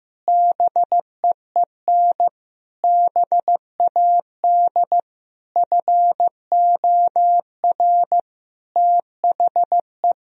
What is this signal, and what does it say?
Morse code, heard as a keyed tone, about 15 wpm, 700 Hz. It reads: BEEN BAD FOR THE
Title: Sentences from Top 1000 Words - Review Part 3 - 15wpm